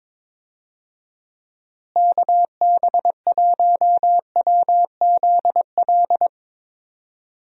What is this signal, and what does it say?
Morse code, heard as a keyed tone, about 22 wpm, 700 Hz. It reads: KB1WZL
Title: USA Call Signs - 6 Characters - Repeated 2x - 22wpm